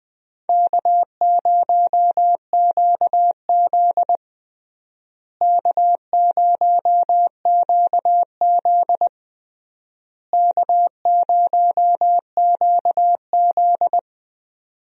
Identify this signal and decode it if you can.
Morse code, heard as a keyed tone, about 20 wpm, 700 Hz. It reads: K0QZ K0QZ K0QZ
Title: USA Call Signs - 4 Characters - Repeated 3x - 20wpm